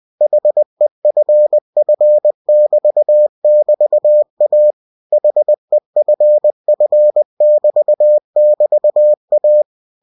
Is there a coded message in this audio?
Yes: Morse code from 0.2 to 9.6 s, heard as a keyed tone, about 20 wpm, 600 Hz. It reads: HEFF<BT><BT>A HEFF<BT><BT>A